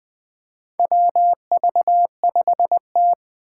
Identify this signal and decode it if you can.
Morse code, heard as a keyed tone, about 21 wpm, 700 Hz. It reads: WV5T
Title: Four Challenging Letter-Number Combinations 20wpm